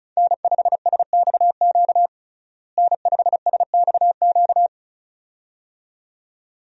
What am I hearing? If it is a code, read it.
Morse code, heard as a keyed tone, about 36 wpm, 700 Hz. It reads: N5SXQ N5SXQ
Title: USA Call Signs - 5 Characters - Repeated 2x - 35wpm